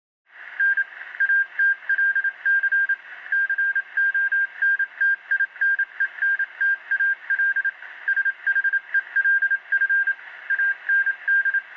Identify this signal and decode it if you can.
Morse code, heard as a keyed tone, about 27 wpm, 1600 Hz. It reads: N ATLC CONTINENTAL SHELF AND